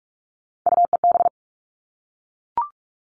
Morse code, heard as a keyed tone, about 44 wpm, 700 Hz. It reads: UEB
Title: All Cut Numbers - 3-Digit 45wpm